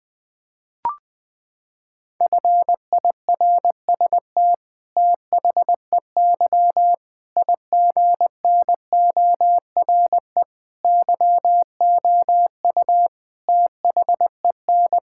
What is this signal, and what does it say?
Morse code, heard as a keyed tone, about 20 wpm, 700 Hz. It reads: FIRST THEY IGNORE YOU THEN
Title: Sets of 8 Words - 20wpm